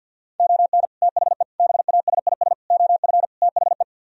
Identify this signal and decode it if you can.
Morse code, heard as a keyed tone, about 60 wpm, 690 Hz. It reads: ON THE BASIS OF THE
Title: Speed-Racing - Sets of 5 Words 50wpm